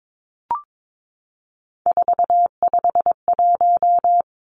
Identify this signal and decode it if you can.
Morse code, heard as a keyed tone, about 22 wpm, 700 Hz. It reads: 451
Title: Three Challenging Letter-Number Combinations 22wpm